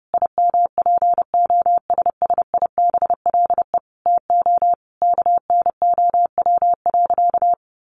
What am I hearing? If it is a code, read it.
Morse code, heard as a keyed tone, about 30 wpm, 700 Hz. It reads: IMPOSSIBLE TO KNOW.
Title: Sentences from Top 1200 Words 30wpm